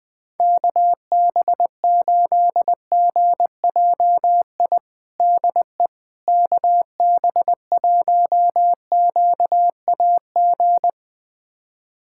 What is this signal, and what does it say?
Morse code, heard as a keyed tone, about 21 wpm, 700 Hz. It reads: KB8GJI DE KB1QAG
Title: USA Callsign DE Callsign 20wpm